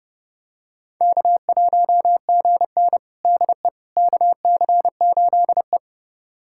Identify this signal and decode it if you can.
Morse code, heard as a keyed tone, about 30 wpm, 700 Hz. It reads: K1GN DE KC8E
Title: USA Callsign DE Callsign - 4 Characters - 30wpm